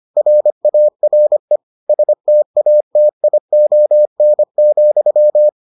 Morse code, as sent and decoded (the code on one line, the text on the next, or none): .-. .- .-. / ... - .- - .. --- -. --..--
RARE STATION,